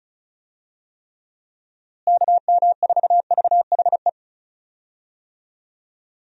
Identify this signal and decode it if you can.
Morse code, heard as a keyed tone, about 36 wpm, 700 Hz. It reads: KM4VHE
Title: USA Call Signs - 6 Characters - Repeated 2x - 35wpm